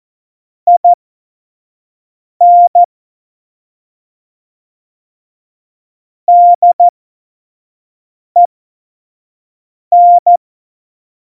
Morse code, heard as a keyed tone, about 14 wpm, spaced as for 5 wpm, 700 Hz. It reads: IN DEN
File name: Morsekurs / Witze / Witz 04. September 2018, 25 F70